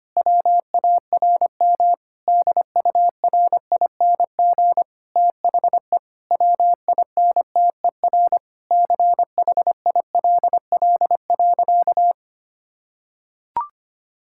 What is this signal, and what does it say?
Morse code, heard as a keyed tone, about 25 wpm, 700 Hz. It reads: WARM DURING THE WINTER CHILL.
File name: practice / 25wpm / Difficult Sentences with Hints - 25wpm